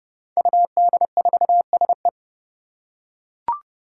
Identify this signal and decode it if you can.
Morse code, heard as a keyed tone, about 30 wpm, 700 Hz. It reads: UD4SE
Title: Five Challenging Letter-Number Combinations 30wpm